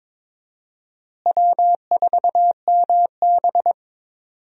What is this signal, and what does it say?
Morse code, heard as a keyed tone, about 22 wpm, 700 Hz. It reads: W4MB
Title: USA Call Signs - 4 Characters - Repeated 2x - 22wpm